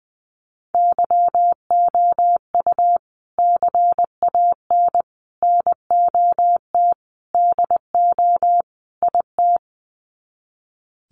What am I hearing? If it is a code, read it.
Morse code, heard as a keyed tone, about 20 wpm, 700 Hz. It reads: YOU CAN NOT DO IT
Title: Sets of 5 Words Encore2 20wpm